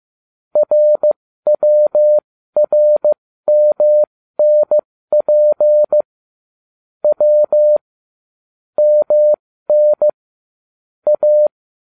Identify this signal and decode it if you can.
Morse code, heard as a keyed tone, about 15 wpm, spaced as for 13 wpm, 600 Hz. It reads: RWRMNP W MN A
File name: cw / ubacw20 2